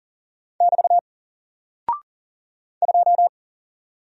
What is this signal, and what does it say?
Morse code, heard as a keyed tone, about 40 wpm, 700 Hz. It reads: <BT> 2
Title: Single Character (Letter:Number:Punctuation:Prosign) Alt 40wpm